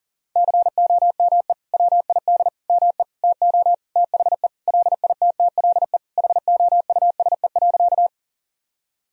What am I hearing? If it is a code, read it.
Morse code, heard as a keyed tone, about 40 wpm, 700 Hz. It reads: COME WID ME TO THE LITTLE HOUSE.
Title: Sentences from Top 200 Words and Abbreviations 40wpm